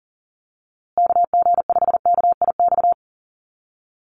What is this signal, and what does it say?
Morse code, heard as a keyed tone, about 40 wpm, 700 Hz. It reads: KG5KIX
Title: USA Call Signs Repeated 2x 40wpm